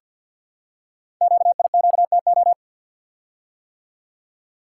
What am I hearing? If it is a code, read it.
Morse code, heard as a keyed tone, about 50 wpm, 690 Hz. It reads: QIQTO
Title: Code Groups - 5 Letters - 50wpm